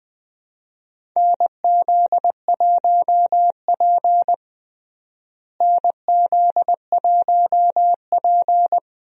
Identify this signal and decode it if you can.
Morse code, heard as a keyed tone, about 20 wpm, 700 Hz. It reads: NZ1P NZ1P
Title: USA Call Signs - 4 Characters - Repeated 3x - 20wpm